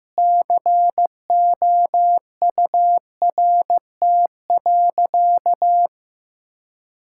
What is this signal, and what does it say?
Morse code, heard as a keyed tone, about 15 wpm, 700 Hz. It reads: COURT.
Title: Sentences from Top 1600 Words 15wpm